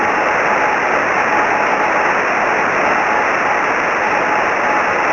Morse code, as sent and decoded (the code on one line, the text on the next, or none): none